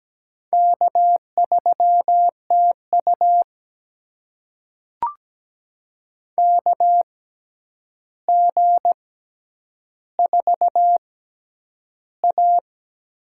Morse code, heard as a keyed tone, about 17 wpm, spaced as for 9 wpm, 700 Hz. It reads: K3TU KG4A
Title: USA Call Signs - 4 Characters - 6x Character Spacing - 17wpm